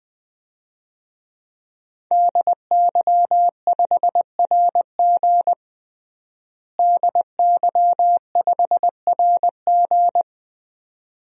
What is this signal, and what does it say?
Morse code, heard as a keyed tone, about 21 wpm, 700 Hz. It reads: DY5RG DY5RG
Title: DXCC-100 Call Signs - 5 Characters - Repeated 2x - 20wpm